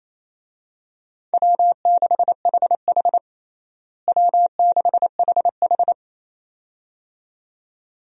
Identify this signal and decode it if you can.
Morse code, heard as a keyed tone, about 28 wpm, 700 Hz. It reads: W6HH W6HH
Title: USA Call Signs - 4 Characters - Repeated 2x - 28wpm